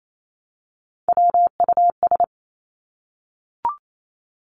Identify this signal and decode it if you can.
Morse code, heard as a keyed tone, about 28 wpm, 700 Hz. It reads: WUS
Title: Three Challenging Letter-Number Combinations 28wpm